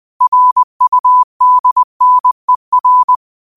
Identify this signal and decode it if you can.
Morse code, heard as a keyed tone, about 20 wpm, 1000 Hz. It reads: RUDNER
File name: qotdfiles / 20wpm